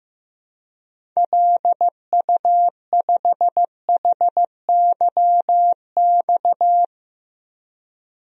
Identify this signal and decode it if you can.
Morse code, heard as a keyed tone, about 15 wpm, 700 Hz. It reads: LU5HYX
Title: DXCC-100 Call Signs - 6 Characters - 15wpm